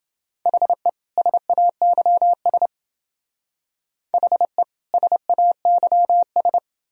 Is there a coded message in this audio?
Yes: Morse code from 0.5 to 6.6 s, heard as a keyed tone, about 28 wpm, 700 Hz. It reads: HE SAYS HE SAYS